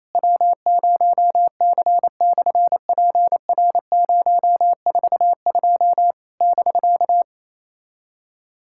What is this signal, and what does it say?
Morse code, heard as a keyed tone, about 27 wpm, 700 Hz. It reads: W0C/PR042 <BK>
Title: SOTA QSO Practice 28wpm